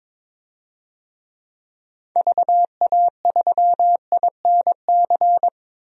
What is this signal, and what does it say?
Morse code, heard as a keyed tone, about 22 wpm, 700 Hz. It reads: VA3INC
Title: DXCC-100 Call Signs - 6 Characters - 22wpm